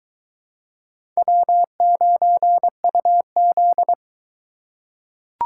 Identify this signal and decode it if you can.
Morse code, heard as a keyed tone, about 20 wpm, 700 Hz. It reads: W9UZ